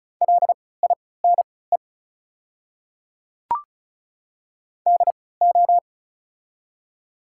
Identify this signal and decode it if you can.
Morse code, heard as a keyed tone, about 36 wpm, spaced as for 20 wpm, 700 Hz. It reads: LINE DO
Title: 3x Character Spacing - Top 300 Words - 35wpm